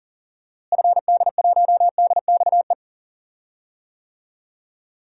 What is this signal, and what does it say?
Morse code, heard as a keyed tone, about 40 wpm, 700 Hz. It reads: FD1DXE
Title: DXCC-100 Call Signs - 6 Characters - Repeated 3x - 40wpm